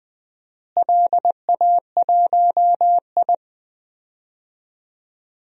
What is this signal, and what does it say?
Morse code, heard as a keyed tone, about 21 wpm, 700 Hz. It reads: LA1I
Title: DXCC-100 Call Signs - 4 Characters - Repeated 2x - 20wpm